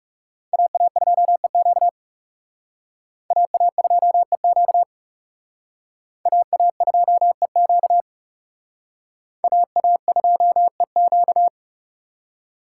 Morse code, heard as a keyed tone, about 37 wpm, 700 Hz. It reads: AA2EQ AA2EQ AA2EQ AA2EQ
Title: Speed Racing - USA 2x2 Call Signs 30wpm